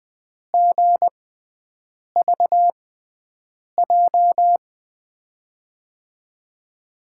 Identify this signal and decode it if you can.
Morse code, heard as a keyed tone, about 21 wpm, spaced as for 7 wpm, 700 Hz. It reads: GVJ
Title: USA Call Signs - 6x Character Spacing - 20wpm